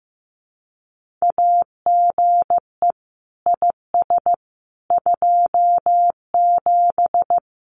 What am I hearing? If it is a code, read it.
Morse code, heard as a keyed tone, about 15 wpm, 700 Hz. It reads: AGE IS 27
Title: Common QSO Phrases Encore 15wpm